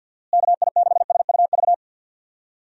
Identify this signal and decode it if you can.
Morse code, heard as a keyed tone, about 51 wpm, 700 Hz. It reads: KI6SUV
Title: USA Call Signs - 6 Characters - Repeated 2x - 50wpm